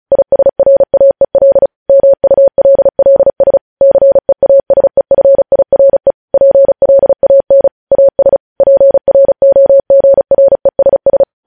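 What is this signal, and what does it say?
Morse code, heard as a keyed tone, about 36 wpm, 550 Hz. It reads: ISRAEL MULLS CEASEFIRE PLAN AS PROGRESS
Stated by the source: Morse code news headlines practice sent at 35 wpm.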